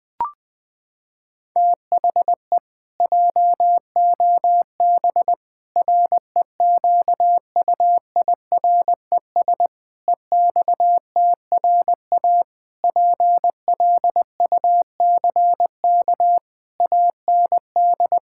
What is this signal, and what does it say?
Morse code, heard as a keyed tone, about 21 wpm, 700 Hz. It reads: THE JOB REQUIRES EXTRA PLUCK AND
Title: Pangram Challenge v2 - 20wpm